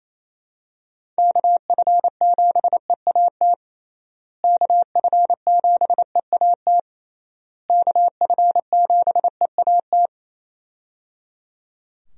Morse code, heard as a keyed tone, about 28 wpm, 700 Hz. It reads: KF7EAT KF7EAT KF7EAT
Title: USA Call Signs - 6 Characters - Repeated 3x - 28wpm